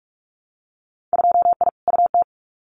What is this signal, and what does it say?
Morse code, heard as a keyed tone, about 44 wpm, 700 Hz. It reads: 2I UT
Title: Winter Field Day Exchanges 45wpm